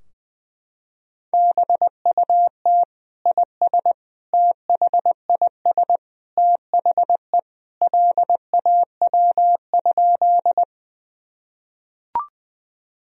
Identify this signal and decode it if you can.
Morse code, heard as a keyed tone, about 21 wpm, 700 Hz. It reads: BUT IS THIS THE LAW?
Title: Sentences from Top 1000 Words - Review Part 3 - 20wpm